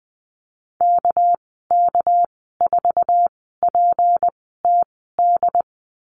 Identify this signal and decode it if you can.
Morse code, heard as a keyed tone, about 20 wpm, spaced as for 14 wpm, 700 Hz. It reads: KK4PTD
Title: 213 Lesson 34g - Number 8 - Call Signs 20wpm